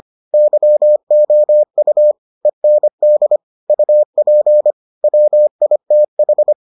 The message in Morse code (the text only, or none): YOU END UP WITH